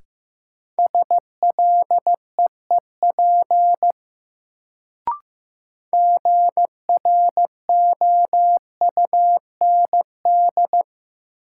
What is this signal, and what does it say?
Morse code, heard as a keyed tone, about 15 wpm, 700 Hz. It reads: SLEEP GROUND
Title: Sentences from Top 500 Words - New Word Drill - 15wpm